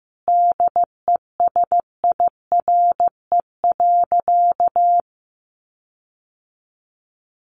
Morse code, heard as a keyed tone, about 15 wpm, 700 Hz. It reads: DESIRE.
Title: Sentences from Top 1300 Words 15wpm